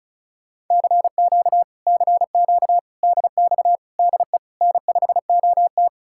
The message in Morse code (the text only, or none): CQ CQ DX DE N5OT